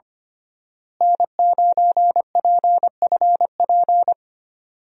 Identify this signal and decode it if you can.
Morse code, heard as a keyed tone, about 25 wpm, 700 Hz. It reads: N9PFP
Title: USA Call Signs - 5 Characters - Repeated 2x - 25wpm